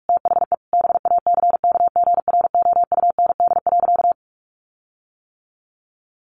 Morse code, heard as a keyed tone, about 44 wpm, 700 Hz. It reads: THE BACKGROUND.
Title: Sentences from Top 1300 Words 45wpm